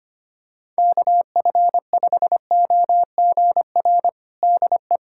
Morse code, heard as a keyed tone, about 25 wpm, 700 Hz. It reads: KF5OGR DE
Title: USA Callsign DE Callsign - 6 Characters - 25wpm